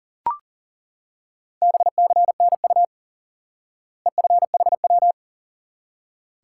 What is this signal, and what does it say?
Morse code, heard as a keyed tone, about 37 wpm, 700 Hz. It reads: BCNU EFHW
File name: IQE Course 3 - Lesson 89 - YAGI - Two Elements - 40wpm